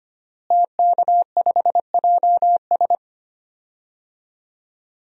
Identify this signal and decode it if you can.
Morse code, heard as a keyed tone, about 25 wpm, 700 Hz. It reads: TK5JS
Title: DXCC-100 Call Signs - 5 Characters - Repeated 3x - 25wpm